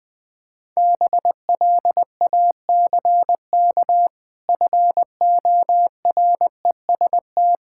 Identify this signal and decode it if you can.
Morse code, heard as a keyed tone, about 20 wpm, 700 Hz. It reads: BLACK FOREST
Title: Difficult Words with Hints - 20wpm